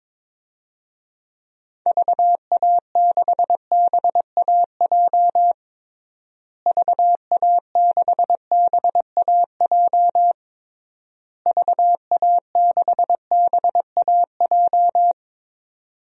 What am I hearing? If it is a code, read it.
Morse code, heard as a keyed tone, about 22 wpm, 700 Hz. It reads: VA6BAJ VA6BAJ VA6BAJ
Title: DXCC-100 Call Signs - 6 Characters - Repeated 3x - 22wpm